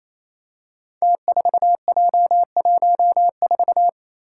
Morse code, heard as a keyed tone, about 28 wpm, 700 Hz. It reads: T4J14
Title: Five Challenging Letter-Number Combinations 28wpm